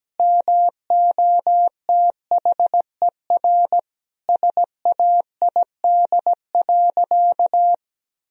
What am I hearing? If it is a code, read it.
Morse code, heard as a keyed tone, about 17 wpm, 700 Hz. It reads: MOTHER SAID.
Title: Sentences from Top 1000 Words - Review Part 4 - 17wpm